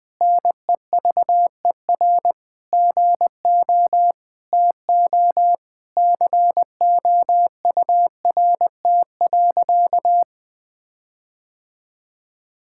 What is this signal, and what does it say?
Morse code, heard as a keyed tone, about 21 wpm, 700 Hz. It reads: NEVER GO TO COURT.
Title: Sentences from Top 1600 Words 20wpm